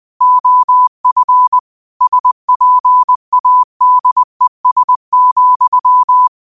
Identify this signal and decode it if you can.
Morse code, heard as a keyed tone, about 20 wpm, 1000 Hz. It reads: OF SPADES,
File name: qotdfiles / 20wpm